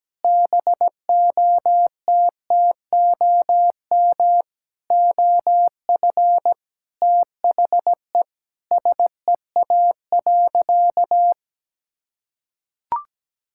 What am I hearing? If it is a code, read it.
Morse code, heard as a keyed tone, about 17 wpm, 700 Hz. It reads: BOTTOM OF THE SEA.
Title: Sentences from Top 1000 Words - Review Part 6 - 17wpm